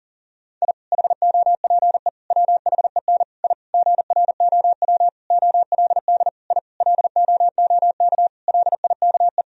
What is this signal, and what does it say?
Morse code, heard as a keyed tone, about 40 wpm, 700 Hz. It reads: I HOPE WHEN I GROW OLD I LOOK LIKE
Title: Sentences from Top 1000 Words - Review Part 1 - 40wpm